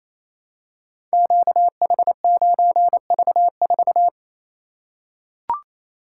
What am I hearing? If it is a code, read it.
Morse code, heard as a keyed tone, about 28 wpm, 700 Hz. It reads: QH9V4
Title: Code Groups - 5 Letter:Numbers - 28wpm